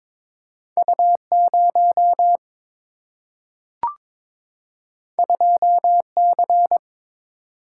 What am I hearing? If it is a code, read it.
Morse code, heard as a keyed tone, about 22 wpm, 700 Hz. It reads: U0 2C
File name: practice / 22wpm / Two Letter-Number Combinations - 22wpm